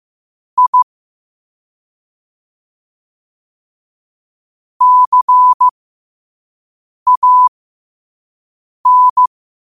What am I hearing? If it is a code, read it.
Morse code, heard as a keyed tone, about 15 wpm, spaced as for 5 wpm, 1000 Hz. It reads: I CAN
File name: qotdfiles / 5wpm